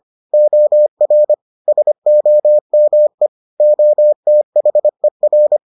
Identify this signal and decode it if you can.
Morse code, heard as a keyed tone, about 25 wpm, 600 Hz. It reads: OR SOME OTHER